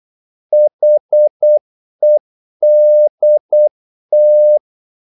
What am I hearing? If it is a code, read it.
Morse code, heard as a keyed tone, about 8 wpm, 600 Hz. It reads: HEDT
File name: Korte ord DK 8wpm